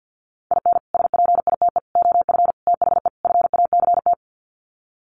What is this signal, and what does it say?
Morse code, heard as a keyed tone, about 48 wpm, 700 Hz. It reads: IN SPITE OF THE FACT
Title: Sets of 5 Words Encore2 50wpm